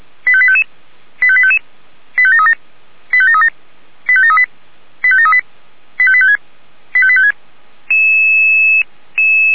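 Five-tone selective call (ZVEI), heard as quick runs of five tones, tones of about 70 ms. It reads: 85855 85855 85737 85737 85738 85738 85856 85856
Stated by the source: ZVEI five-tone selective call 85853 received on an SDR.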